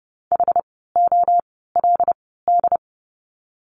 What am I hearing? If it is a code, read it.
Morse code, heard as a keyed tone, about 30 wpm, spaced as for 17 wpm, 700 Hz. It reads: HOLD
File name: Lesson 17f - Letter u - Words 30wpm